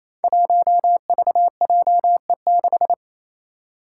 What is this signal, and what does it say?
Morse code, heard as a keyed tone, about 28 wpm, 700 Hz. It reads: 1VJE6
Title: Five Challenging Letter-Number Combinations 28wpm